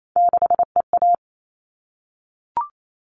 Morse code, heard as a keyed tone, about 28 wpm, 700 Hz. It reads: 6EA
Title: All Cut Numbers - 3-Digit 28wpm